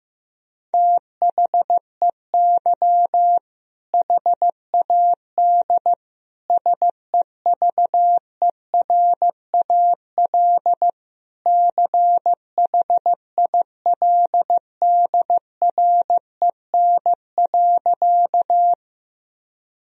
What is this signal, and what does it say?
Morse code, heard as a keyed tone, about 15 wpm, 700 Hz. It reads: THEY HAD SEVERAL CHILDREN.
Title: Sentences from Top 1000 Words - Review Part 2 - 15wpm